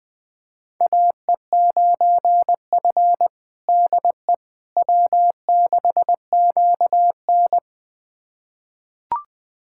Morse code, heard as a keyed tone, about 20 wpm, 700 Hz. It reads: AE9F DE W6QN
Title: USA Callsign DE Callsign - 4 Characters - 20wpm